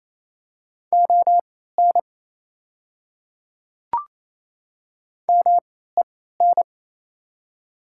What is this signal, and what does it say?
Morse code, heard as a keyed tone, about 28 wpm, spaced as for 16 wpm, 700 Hz. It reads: ON MEN
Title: 3x Character Spacing - Top 200 Words - 28wpm